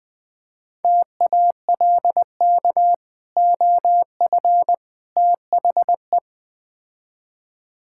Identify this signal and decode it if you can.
Morse code, heard as a keyed tone, about 20 wpm, 700 Hz. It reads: TALK OF THE
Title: Build A Phrase - 4-Word Phrases - 20wpm